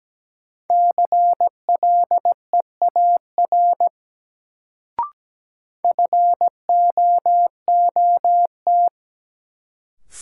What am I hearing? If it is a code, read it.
Morse code, heard as a keyed tone, about 17 wpm, 700 Hz. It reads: CLEAR FOOT
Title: Sentences from Top 500 Words - New Word Drill - 17wpm